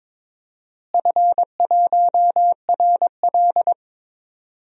Morse code, heard as a keyed tone, about 21 wpm, 700 Hz. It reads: F1RL
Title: DXCC-100 Call Signs - 4 Characters - Repeated 2x - 22wpm